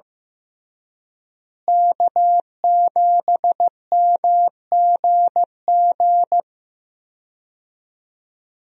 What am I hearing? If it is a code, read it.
Morse code, heard as a keyed tone, about 16 wpm, 700 Hz. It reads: K7MGG
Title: USA Call Signs - 5 Characters - Repeated 3x - 15wpm